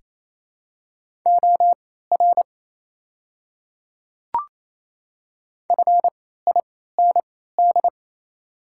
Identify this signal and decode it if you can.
Morse code, heard as a keyed tone, about 28 wpm, spaced as for 16 wpm, 700 Hz. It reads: OR FIND
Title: 3x Character Spacing - Top 500 Words - 28wpm